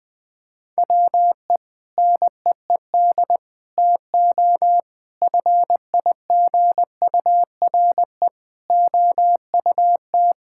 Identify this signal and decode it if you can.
Morse code, heard as a keyed tone, about 20 wpm, 700 Hz. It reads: WE NEED TO FIGURE OUT
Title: Sentences with 6-Word Phrases - 20wpm